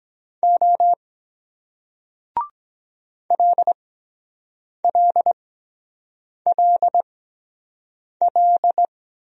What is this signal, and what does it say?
Morse code, heard as a keyed tone, about 23 wpm, spaced as for 7 wpm, 700 Hz. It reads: O LLLL